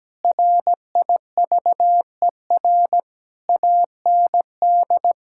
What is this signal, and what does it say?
Morse code, heard as a keyed tone, about 17 wpm, 700 Hz. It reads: RIVER AND
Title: Sentences from Top 1000 Words - Review Part 6 - 17wpm